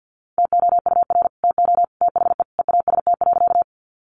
Morse code, heard as a keyed tone, about 50 wpm, 700 Hz. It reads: TOUR TO THE EAST.
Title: Sentences from Top 1400 Words 50wpm